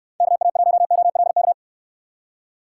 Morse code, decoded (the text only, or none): KA1OLK